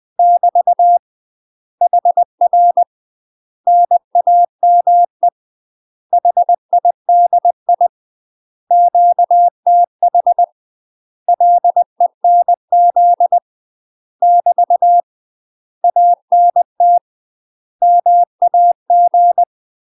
Morse code, heard as a keyed tone, about 20 wpm, 700 Hz. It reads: <BT> HR NAME HIDI QTH LENZ <BT> ANT MAG
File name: UK MASTER QSOs TRAINING / 20WPM / UK MASTER QSOs 0055